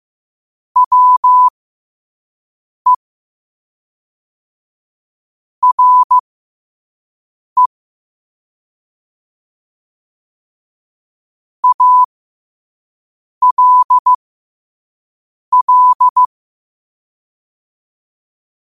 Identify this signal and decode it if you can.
Morse code, heard as a keyed tone, about 15 wpm, spaced as for 5 wpm, 1000 Hz. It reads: WE RE ALL
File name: qotdfiles / 5wpm